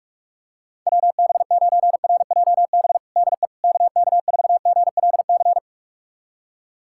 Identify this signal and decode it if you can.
Morse code, heard as a keyed tone, about 46 wpm, 700 Hz. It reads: WB9RJB DE KK4GLC
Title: USA Callsign DE Callsign 45wpm